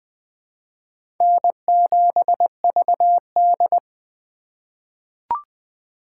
Morse code, heard as a keyed tone, about 20 wpm, 700 Hz. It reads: N7VD
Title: USA Call Signs - 4x Character Spacing - 20wpm